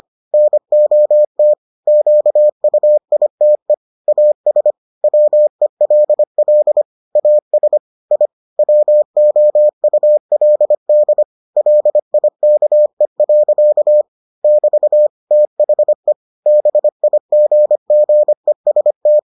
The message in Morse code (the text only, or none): NOT QUITE AS WELL AS I WOULD LIKE. <BT> THE BIGGEST